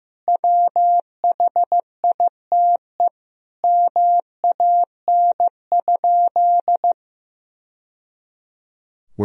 Morse code, heard as a keyed tone, about 15 wpm, 700 Hz. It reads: WHITE MAN?
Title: Sentences from Top 1000 Words - Review Part 1 - 15wpm